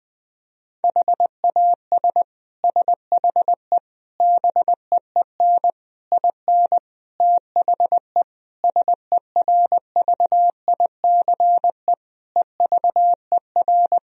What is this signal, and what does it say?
Morse code, heard as a keyed tone, about 20 wpm, 700 Hz. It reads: HAS SHE BEEN IN THE SERVICE EVER
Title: Sentences from Top 1000 Words - Review Part 4 - 20wpm